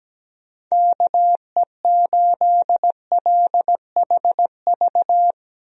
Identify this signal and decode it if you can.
Morse code, heard as a keyed tone, about 17 wpm, 700 Hz. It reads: KE8LHV
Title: USA Call Signs - 6 Characters - Repeated 3x - 17wpm